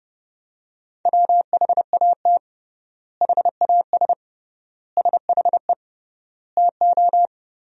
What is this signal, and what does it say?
Morse code, heard as a keyed tone, about 30 wpm, 700 Hz. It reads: WHAT HAS SHE TO
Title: 3x Word-Spacing - Sentences from Top 100 Words - 30wpm